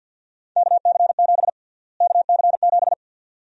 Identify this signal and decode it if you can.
Morse code, heard as a keyed tone, about 50 wpm, 700 Hz. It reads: KC7 KC7
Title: USA Call Sign Prefixes - Repeated 2x - 50wpm